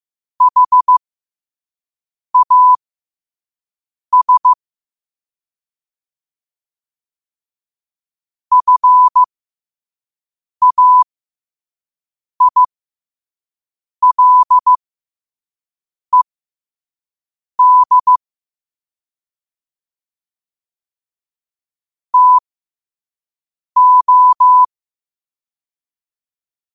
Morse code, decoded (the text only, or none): HAS FAILED TO